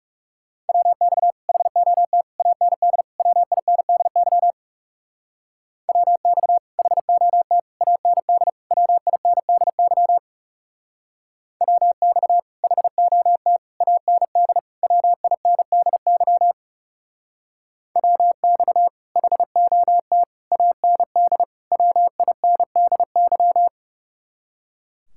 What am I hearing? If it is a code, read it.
Morse code, heard as a keyed tone, about 39 wpm, 700 Hz. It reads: WX HOT AND WINDY WX HOT AND WINDY WX HOT AND WINDY WX HOT AND WINDY